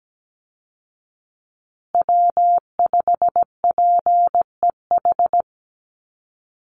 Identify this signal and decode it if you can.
Morse code, heard as a keyed tone, about 17 wpm, 700 Hz. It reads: W5PEH
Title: Call Signs Repeated 3x - 17wpm